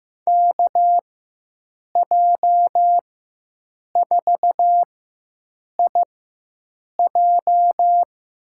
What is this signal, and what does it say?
Morse code, heard as a keyed tone, about 15 wpm, spaced as for 7 wpm, 700 Hz. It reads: KJ4IJ
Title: USA Call Signs - 5 Characters - 4x Character Spacing - 15wpm